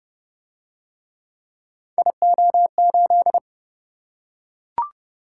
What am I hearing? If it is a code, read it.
Morse code, heard as a keyed tone, about 28 wpm, 700 Hz. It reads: IO8